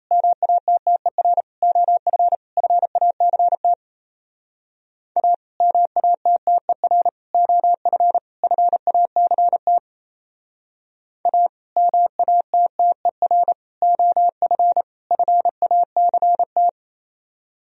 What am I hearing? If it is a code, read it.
Morse code, heard as a keyed tone, about 35 wpm, 700 Hz. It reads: MATTER OF FACT A MATTER OF FACT A MATTER OF FACT
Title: Speed-Racing - Sets of 4 Words 28wpm